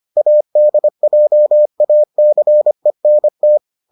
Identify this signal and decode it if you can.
Morse code, heard as a keyed tone, about 25 wpm, 600 Hz. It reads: ADJACENT